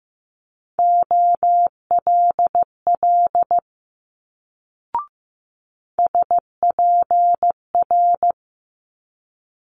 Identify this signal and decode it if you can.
Morse code, heard as a keyed tone, about 15 wpm, 700 Hz. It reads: OLL SPR
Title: Multi-letter Phonemes Quick Fire 15wpm